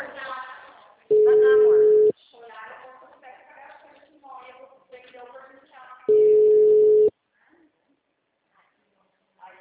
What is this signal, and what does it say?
Ringback tone (Europe, 425 Hz), with speech.